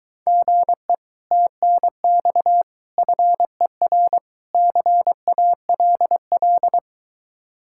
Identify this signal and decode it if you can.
Morse code, heard as a keyed tone, about 23 wpm, 700 Hz. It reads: GE TNX FER CALL